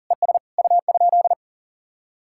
Morse code, heard as a keyed tone, about 50 wpm, 700 Hz. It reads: ES U?